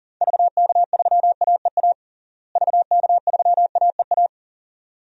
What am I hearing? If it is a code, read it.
Morse code, heard as a keyed tone, about 40 wpm, 700 Hz. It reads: VK3AEA VK3AEA
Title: DXCC-100 Call Signs - 6 Characters - Repeated 2x - 40wpm